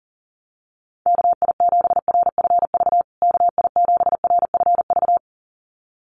Morse code, heard as a keyed tone, about 40 wpm, 700 Hz. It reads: KI7RFV KI7RFV
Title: USA Call Signs Repeated 2x 40wpm